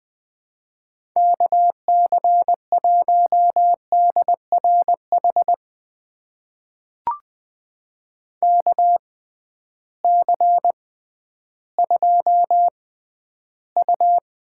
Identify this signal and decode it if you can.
Morse code, heard as a keyed tone, about 21 wpm, spaced as for 12 wpm, 700 Hz. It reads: KC1DRH KC2U
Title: USA Call Signs - 6 Characters - 6x Character Spacing - 20wpm